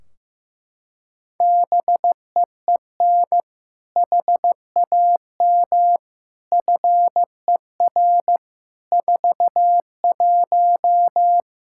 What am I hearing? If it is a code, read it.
Morse code, heard as a keyed tone, about 15 wpm, 700 Hz. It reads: BEEN HAM FER 41
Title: K1USN SST QSO Practice 15wpm